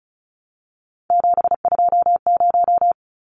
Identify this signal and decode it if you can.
Morse code, heard as a keyed tone, about 34 wpm, 700 Hz. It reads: 720
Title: Numbers - 3-Digit 35wpm